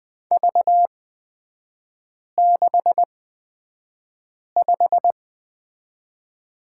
Morse code, heard as a keyed tone, about 20 wpm, spaced as for 5 wpm, 700 Hz. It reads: V65
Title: 243 Lesson 39e - Letter z - Three Characters 20wpm